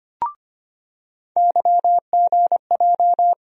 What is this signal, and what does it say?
Morse code, heard as a keyed tone, about 25 wpm, 700 Hz. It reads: YGJ
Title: USA Call Sign Postfixes - 3 Letters 25wpm